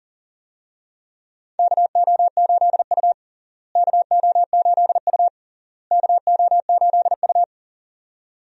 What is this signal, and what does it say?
Morse code, heard as a keyed tone, about 40 wpm, 700 Hz. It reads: KO8U KO8U KO8U
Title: USA Call Signs - 4 Characters - Repeated 3x - 40wpm